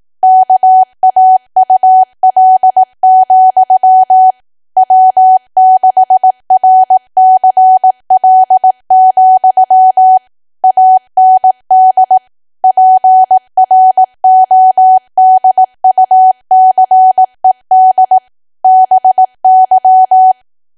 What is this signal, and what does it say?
Morse code, heard as a keyed tone, about 18 wpm, 750 Hz. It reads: KAUL, W6RCL, AND PRODUCED BY
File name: Archive / 18 WPM / 18WPM